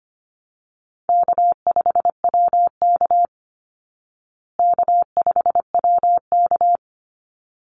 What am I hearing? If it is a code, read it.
Morse code, heard as a keyed tone, about 25 wpm, 700 Hz. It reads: K5WK K5WK
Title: Easy Call Signs Repeated 2x - 25wpm